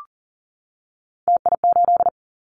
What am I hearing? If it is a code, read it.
Morse code, heard as a keyed tone, about 38 wpm, 700 Hz. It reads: TI8
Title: Callsign Prefixes for Top 100 DXCC - 40wpm